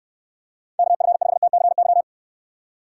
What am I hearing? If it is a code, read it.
Morse code, heard as a keyed tone, about 69 wpm, 700 Hz. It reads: KF5TCY